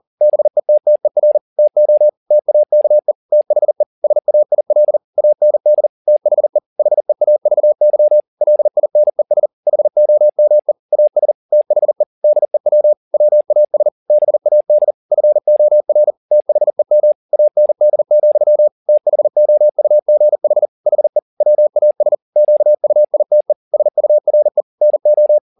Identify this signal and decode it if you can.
Morse code, heard as a keyed tone, about 40 wpm, 600 Hz. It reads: BETTER TO TAKE THE SAIL AND THE HEAVY LINES HOME AS THE DEW WAS BAD FOR THEM AND, THOUGH HE WAS QUITE SURE NO